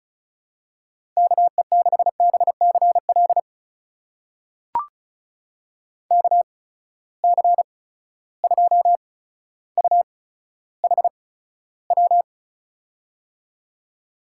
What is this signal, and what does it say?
Morse code, heard as a keyed tone, about 36 wpm, spaced as for 15 wpm, 700 Hz. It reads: KE6BCL KC2UHW
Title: USA Call Signs - 6 Characters - 8x Character Spacing - 35wpm